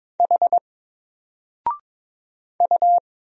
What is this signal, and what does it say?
Morse code, heard as a keyed tone, about 22 wpm, 700 Hz. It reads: H U
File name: Single-Letter-Number-22wpm